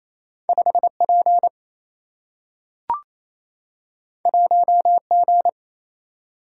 Morse code, heard as a keyed tone, about 28 wpm, 700 Hz. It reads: 5P 1G
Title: Single-Letter Single-Number 28wpm